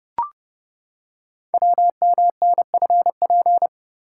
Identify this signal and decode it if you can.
Morse code, heard as a keyed tone, about 30 wpm, 690 Hz. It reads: WMNFP
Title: Code Groups - 5 Letters - 30wpm